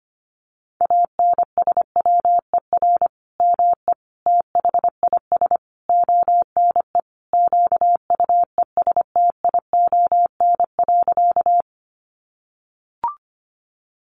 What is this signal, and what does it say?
Morse code, heard as a keyed tone, about 25 wpm, 700 Hz. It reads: ANSWER ME THIS ONE QUESTION.
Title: Sentences from Top 400 Words 25wpm